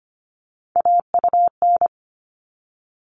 Morse code, heard as a keyed tone, about 25 wpm, 700 Hz. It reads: AUN
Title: All Cut Numbers - 3-Digit 25wpm